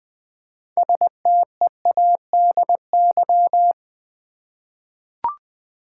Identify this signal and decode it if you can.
Morse code, heard as a keyed tone, about 20 wpm, 700 Hz. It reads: STEADY
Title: 6 Letter Words - First 3 Letters Given - 20wpm